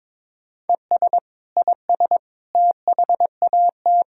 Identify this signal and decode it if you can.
Morse code, heard as a keyed tone, about 22 wpm, 700 Hz. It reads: ES IS THAT
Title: Sentences from Top 100 Words and Abbreviations 22wpm